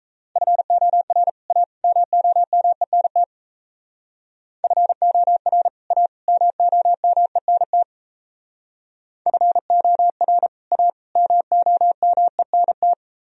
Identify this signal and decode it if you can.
Morse code, heard as a keyed tone, about 34 wpm, 700 Hz. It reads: FOR A MOMENT FOR A MOMENT FOR A MOMENT
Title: Speed-Racing - Sets of 3 Words 28wpm